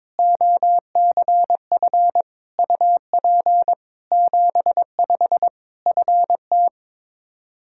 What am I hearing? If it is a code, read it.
Morse code, heard as a keyed tone, about 22 wpm, 700 Hz. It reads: OCF UP 75 FT
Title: K1USN SST QSO Practice 22wpm